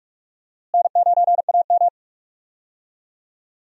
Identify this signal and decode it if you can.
Morse code, heard as a keyed tone, about 42 wpm, 700 Hz. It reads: N9AM
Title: USA Call Signs - 4 Characters - Repeated 2x - 45wpm